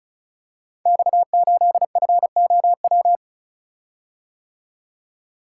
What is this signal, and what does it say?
Morse code, heard as a keyed tone, about 36 wpm, 690 Hz. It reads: X8FOW
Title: Code Groups - 5 Letter:Numbers - 35wpm